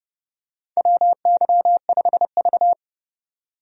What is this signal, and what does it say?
Morse code, heard as a keyed tone, about 29 wpm, 700 Hz. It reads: WY5V
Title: USA Call Signs - 4 Characters - 30wpm